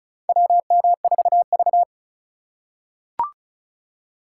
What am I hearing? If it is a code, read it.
Morse code, heard as a keyed tone, about 36 wpm, 700 Hz. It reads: WM4V
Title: USA Call Signs - 4 Characters - 4x Character Spacing - 35wpm